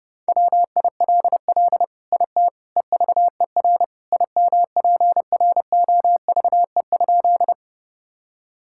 Morse code, heard as a keyed tone, about 30 wpm, 700 Hz. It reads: WILL IT EVER IMPROVE?
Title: Sentences from Top 1000 Words - Review Part 5 - 30wpm